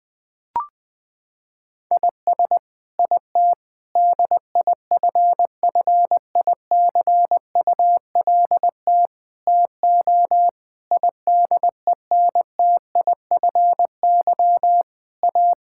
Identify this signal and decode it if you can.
Morse code, heard as a keyed tone, about 20 wpm, 700 Hz. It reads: IS IT DIFFICULT TO IDENTIFY A
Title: Sentences from Top 1000 Words - Review Part 6 - 20wpm